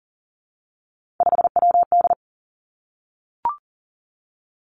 Morse code, heard as a keyed tone, about 41 wpm, 690 Hz. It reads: 5WD
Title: Three Challenging Letter-Number Combinations 40wpm